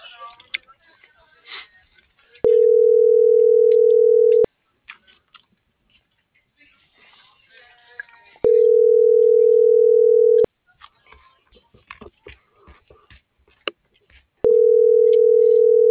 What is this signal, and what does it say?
Ringback tone (North America), 440 + 480 Hz.